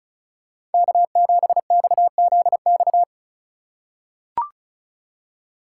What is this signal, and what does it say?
Morse code, heard as a keyed tone, about 34 wpm, 700 Hz. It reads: K7XZX
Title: USA Call Signs - 5 Characters - 4x Character Spacing - 35wpm